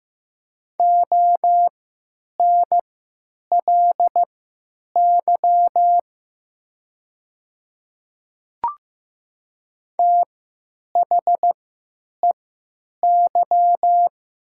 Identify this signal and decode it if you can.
Morse code, heard as a keyed tone, about 15 wpm, spaced as for 9 wpm, 700 Hz. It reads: ONLY THEY
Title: 3x Character Spacing - Top 500 Words - 15wpm